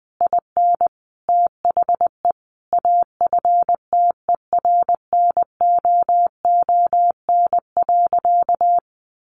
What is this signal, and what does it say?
Morse code, heard as a keyed tone, about 20 wpm, 700 Hz. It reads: IN THE AFTERNOON.
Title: Sentences from Top 1500 Words 20wpm